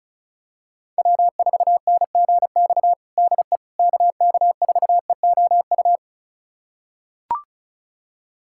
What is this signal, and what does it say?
Morse code, heard as a keyed tone, about 34 wpm, 700 Hz. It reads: W4NGX DE KK4EOU
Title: USA Callsign DE Callsign 35wpm